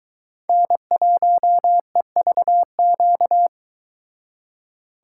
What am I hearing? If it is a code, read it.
Morse code, heard as a keyed tone, about 23 wpm, 700 Hz. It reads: N1EVQ